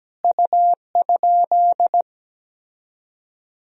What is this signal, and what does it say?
Morse code, heard as a keyed tone, about 17 wpm, 700 Hz. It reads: U?
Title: Sentences from Top 100 Words and Abbreviations 17wpm